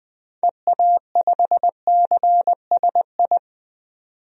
Morse code, heard as a keyed tone, about 21 wpm, 700 Hz. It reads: EA5CSI
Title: DXCC-100 Call Signs - 6 Characters - 20wpm